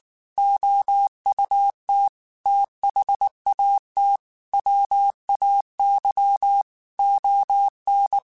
Morse code, heard as a keyed tone, about 19 wpm, 800 Hz. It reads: OUT THAT WAY ON
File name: QSOs / QSOs-mp3s / WA7QOX 1@18